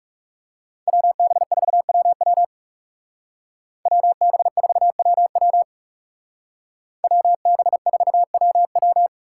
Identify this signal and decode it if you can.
Morse code, heard as a keyed tone, about 39 wpm, 700 Hz. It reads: WB4WW WB4WW WB4WW